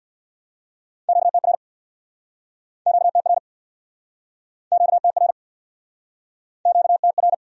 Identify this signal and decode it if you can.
Morse code, heard as a keyed tone, about 69 wpm, 700 Hz. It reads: QTR QTR QTR QTR